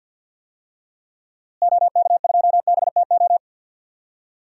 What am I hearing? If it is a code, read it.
Morse code, heard as a keyed tone, about 53 wpm, 700 Hz. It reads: OK2BTO
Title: DXCC-100 Call Signs - 6 Characters - Repeated 2x - 50wpm